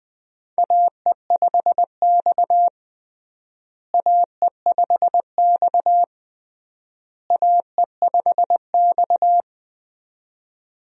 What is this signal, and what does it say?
Morse code, heard as a keyed tone, about 21 wpm, 700 Hz. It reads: AE5X AE5X AE5X
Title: USA Call Signs - 4 Characters - Repeated 3x - 20wpm